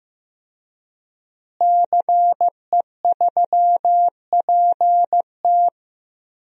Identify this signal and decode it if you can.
Morse code, heard as a keyed tone, about 15 wpm, 700 Hz. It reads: CE3PT
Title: DXCC-100 Call Signs - 5 Characters - Repeated 2x - 15wpm